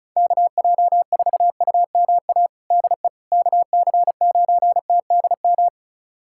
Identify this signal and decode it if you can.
Morse code, heard as a keyed tone, about 36 wpm, 700 Hz. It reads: KJ4UMA DE KC9TDM
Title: USA Callsign DE Callsign - 6 Characters - 35wpm